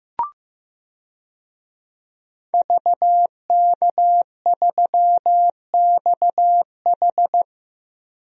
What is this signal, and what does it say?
Morse code, heard as a keyed tone, about 15 wpm, 700 Hz. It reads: VK3XH
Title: DXCC-100 Call Signs - 5 Characters - Repeated 3x - 15wpm